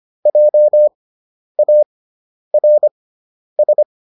Morse code, heard as a keyed tone, about 25 wpm, spaced as for 10 wpm, 600 Hz. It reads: JARS